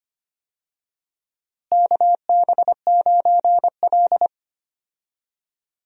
Morse code, heard as a keyed tone, about 25 wpm, 700 Hz. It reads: KB9L
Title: USA Call Signs - 4 Characters - Repeated 2x - 25wpm